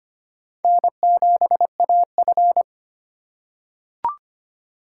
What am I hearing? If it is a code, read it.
Morse code, heard as a keyed tone, about 25 wpm, 700 Hz. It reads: N7AF
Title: USA Call Signs - 4 Characters - 3x Character Spacing - 25wpm